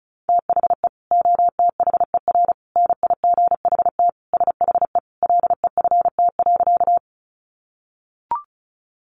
Morse code, heard as a keyed tone, about 34 wpm, 700 Hz. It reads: THE OTHER NIGHT SHE LEFT.
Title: Sentences from Top 300 Words 35wpm